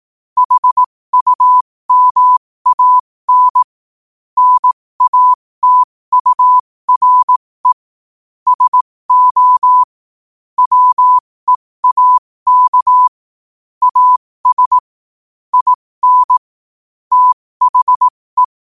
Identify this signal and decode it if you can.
Morse code, heard as a keyed tone, about 18 wpm, spaced as for 15 wpm, 1000 Hz. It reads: HUMAN NATURE SO WEAK AS IN THE
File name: qotdfiles / 15wpm